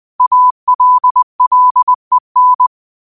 Morse code, heard as a keyed tone, about 20 wpm, 1000 Hz. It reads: ALLEN
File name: qotdfiles / 20wpm